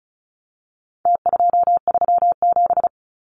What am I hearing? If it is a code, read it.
Morse code, heard as a keyed tone, about 34 wpm, 700 Hz. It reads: T237
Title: Numbers - Contest Style - 4-Digit 35wpm